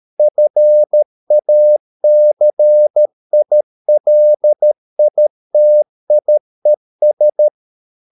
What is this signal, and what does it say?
Morse code, heard as a keyed tone, about 13 wpm, 600 Hz. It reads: FACILITIES